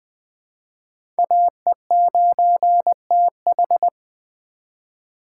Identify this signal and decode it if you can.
Morse code, heard as a keyed tone, about 20 wpm, 700 Hz. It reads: AE9TH
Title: USA Call Signs - 2x Character Spacing - 20wpm